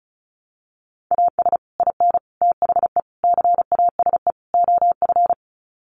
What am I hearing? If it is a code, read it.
Morse code, heard as a keyed tone, about 34 wpm, 700 Hz. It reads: AS IN THE CASE OF
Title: Sets of 5 Words 35wpm